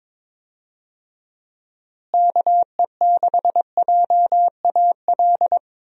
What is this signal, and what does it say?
Morse code, heard as a keyed tone, about 23 wpm, 700 Hz. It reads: KE6JAL
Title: USA Call Signs - 6 Characters - Repeated 2x - 22wpm